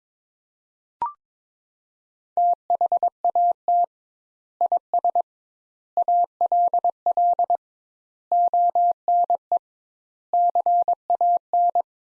Morse code, heard as a keyed tone, about 21 wpm, 700 Hz. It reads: THAT IS ALL ONE CAN
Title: Sentences Top 200 Words - 22wpm